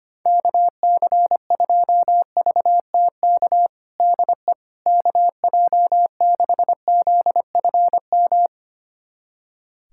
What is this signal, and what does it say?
Morse code, heard as a keyed tone, about 25 wpm, 700 Hz. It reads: KC2VTK DE KJ6ZFM
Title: USA Callsign DE Callsign 25wpm